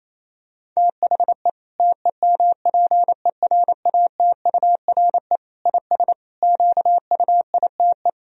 Morse code, heard as a keyed tone, about 28 wpm, 700 Hz. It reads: THE TEMPERATURE IS QUITE
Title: Sentences from Top 1000 Words - Review Part 4 - 28wpm